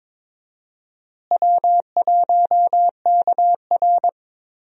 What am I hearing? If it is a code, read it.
Morse code, heard as a keyed tone, about 22 wpm, 700 Hz. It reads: W1KR
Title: USA Call Signs - 4 Characters - Repeated 3x - 22wpm